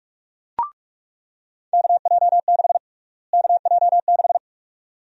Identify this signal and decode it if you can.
Morse code, heard as a keyed tone, about 41 wpm, 700 Hz. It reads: KJ6 KJ6
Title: USA Call Sign Prefixes - Repeated 2x - 45wpm